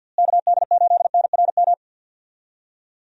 Morse code, heard as a keyed tone, about 47 wpm, 700 Hz. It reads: KD8NRM
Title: USA Call Signs - 6 Characters - Repeated 2x - 50wpm